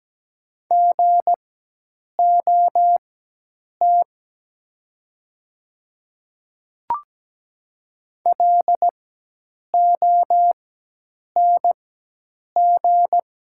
Morse code, heard as a keyed tone, about 17 wpm, spaced as for 8 wpm, 700 Hz. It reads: GOT LONG
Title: 4x Character Spacing - Top 300 Words - 17wpm